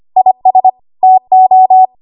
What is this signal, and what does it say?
Morse code, heard as a keyed tone, about 25 wpm, 750 Hz. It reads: IS TO
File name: Archive / 25 WPM / 25WPM